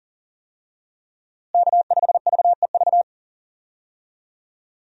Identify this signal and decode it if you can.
Morse code, heard as a keyed tone, about 40 wpm, 700 Hz. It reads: K5VEV
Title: USA Call Signs - 5 Characters - Repeated 3x - 40wpm